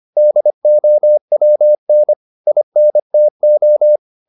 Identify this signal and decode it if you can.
Morse code, heard as a keyed tone, about 25 wpm, 600 Hz. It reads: DOWN INTO
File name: the old man and the sea part 1 25wpm